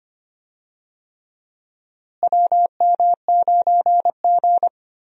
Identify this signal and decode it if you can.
Morse code, heard as a keyed tone, about 25 wpm, 700 Hz. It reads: WM9G
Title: USA Call Signs - 4 Characters - Repeated 3x - 25wpm